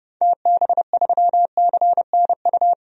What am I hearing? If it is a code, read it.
Morse code, heard as a keyed tone, about 30 wpm, 700 Hz. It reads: TB3CNU
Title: DXCC-100 Call Signs - 6 Characters - 30wpm